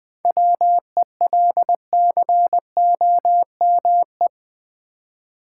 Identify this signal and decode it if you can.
Morse code, heard as a keyed tone, about 20 wpm, 700 Hz. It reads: WELCOME
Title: 7 Letter Words - First 3 Letters Given - 20wpm